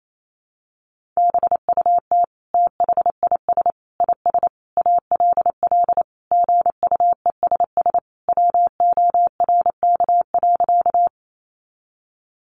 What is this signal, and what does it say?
Morse code, heard as a keyed tone, about 28 wpm, 700 Hz. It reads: BUT THIS IS ALL GUESS WORK.
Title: Sentences from Top 1500 Words 28wpm